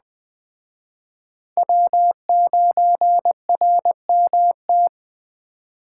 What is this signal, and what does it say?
Morse code, heard as a keyed tone, about 21 wpm, 700 Hz. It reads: W9RMT